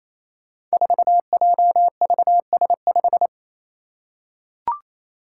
Morse code, heard as a keyed tone, about 27 wpm, 700 Hz. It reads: 4JVS5
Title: Five Challenging Letter-Number Combinations 28wpm